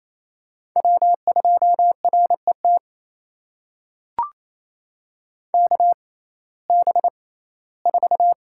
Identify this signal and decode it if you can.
Morse code, heard as a keyed tone, about 28 wpm, spaced as for 17 wpm, 700 Hz. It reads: W2RET KB4
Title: USA Call Signs - 6x Character Spacing - 28wpm